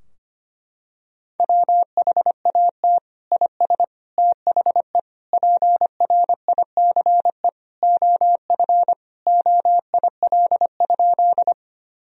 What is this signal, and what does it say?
Morse code, heard as a keyed tone, about 25 wpm, 700 Hz. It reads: WHAT IS THE PRICE OF OIL?
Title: Sentences from Top 1000 Words - Review Part 4 - 25wpm